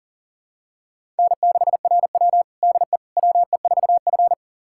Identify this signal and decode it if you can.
Morse code, heard as a keyed tone, about 40 wpm, 700 Hz. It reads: N6RW DE WE4F
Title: USA Callsign DE Callsign - 4 Characters - 40wpm